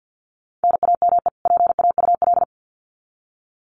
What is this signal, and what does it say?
Morse code, heard as a keyed tone, about 50 wpm, 700 Hz. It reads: NAME PAUL
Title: Common QSO Phrases 50wpm